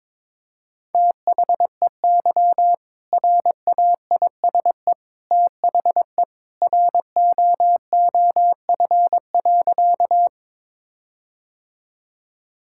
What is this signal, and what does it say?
Morse code, heard as a keyed tone, about 22 wpm, 700 Hz. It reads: THEY RAISE THE ROOF.
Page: morse-code-ninja-practice-22wpm